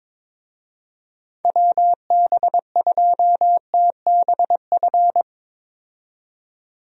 Morse code, heard as a keyed tone, about 22 wpm, 700 Hz. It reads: WB2TBF
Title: USA Call Signs - 6 Characters - 22wpm